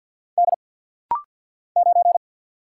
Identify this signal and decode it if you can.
Morse code, heard as a keyed tone, about 50 wpm, 700 Hz. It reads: D 9
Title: Single Letter-Number Alt - Rapid-Fire 50wpm